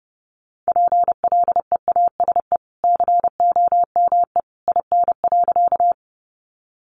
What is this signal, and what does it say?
Morse code, heard as a keyed tone, about 30 wpm, 700 Hz. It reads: PLEASE COME IN.
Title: Sentences from Top 1500 Words 30wpm